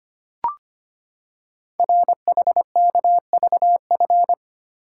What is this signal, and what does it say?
Morse code, heard as a keyed tone, about 25 wpm, 700 Hz. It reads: RHKVF
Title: Code Groups - 5 Letters - 25wpm